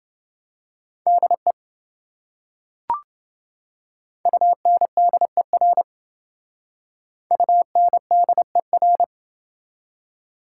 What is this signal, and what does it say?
Morse code, heard as a keyed tone, about 28 wpm, 700 Hz. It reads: DE UNDER UNDER